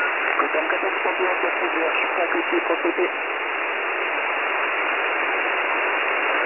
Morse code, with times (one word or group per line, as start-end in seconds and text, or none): none